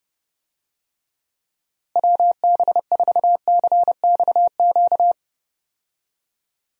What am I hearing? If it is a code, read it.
Morse code, heard as a keyed tone, about 30 wpm, 700 Hz. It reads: WB4CXQ
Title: USA Call Signs - 6 Characters - Repeated 2x - 30wpm